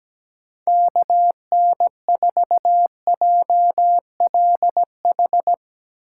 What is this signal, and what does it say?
Morse code, heard as a keyed tone, about 19 wpm, 700 Hz. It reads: KN4JLH
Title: Speed Racing - USA 2x3 Call Signs 17wpm